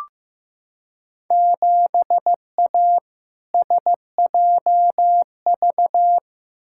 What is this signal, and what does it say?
Morse code, heard as a keyed tone, about 15 wpm, 700 Hz. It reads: 7A SJV
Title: Field Day Exchanges Encore - 15wpm